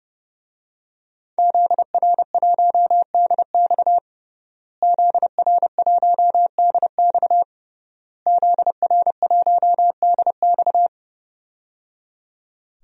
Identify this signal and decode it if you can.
Morse code, heard as a keyed tone, about 30 wpm, 700 Hz. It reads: ZR1DX ZR1DX ZR1DX
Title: DXCC-100 Call Signs - 5 Characters - Repeated 3x - 30wpm